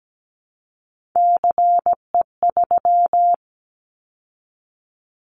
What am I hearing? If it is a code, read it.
Morse code, heard as a keyed tone, about 17 wpm, 700 Hz. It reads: CE3
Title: Callsign Prefixes for Top 100 DXCC - 17wpm